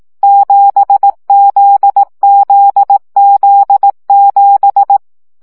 Morse code, heard as a keyed tone, about 18 wpm, 800 Hz. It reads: 7ZZZ7
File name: CW gyakorlas z 7 vesszo 15 baud